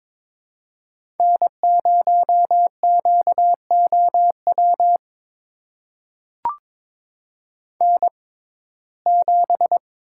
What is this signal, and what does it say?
Morse code, heard as a keyed tone, about 22 wpm, spaced as for 16 wpm, 700 Hz. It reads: N0QOW N7
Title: USA Call Signs - 5 Characters - 6x Character Spacing - 22wpm